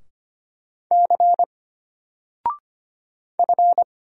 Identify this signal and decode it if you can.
Morse code, heard as a keyed tone, about 25 wpm, 700 Hz. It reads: C F
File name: Single-Letter-Number-25wpm